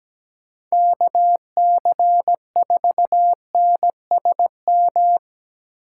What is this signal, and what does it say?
Morse code, heard as a keyed tone, about 17 wpm, 700 Hz. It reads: KC4NSM
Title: USA Call Signs - 6 Characters - 4x Character Spacing - 17wpm